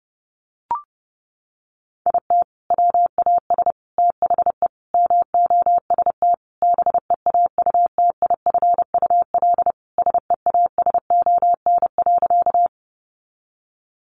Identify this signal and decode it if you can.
Morse code, heard as a keyed tone, about 30 wpm, 700 Hz. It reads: IT WAS THE MOST BEAUTIFUL SEASON.
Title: Sentences from Top 1400 Words 30wpm